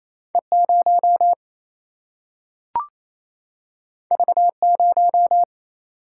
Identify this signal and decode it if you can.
Morse code, heard as a keyed tone, about 28 wpm, 700 Hz. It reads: E 0 V 0
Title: Single-Letter Single-Number 28wpm